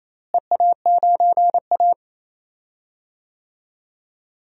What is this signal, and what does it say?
Morse code, heard as a keyed tone, about 28 wpm, 700 Hz. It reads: EA9A
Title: DXCC-100 Call Signs - 4 Characters - Repeated 2x - 28wpm